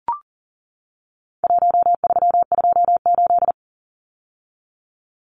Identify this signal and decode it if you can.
Morse code, heard as a keyed tone, about 40 wpm, 700 Hz. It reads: 1328
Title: Numbers - Contest Style - 4-Digit 40wpm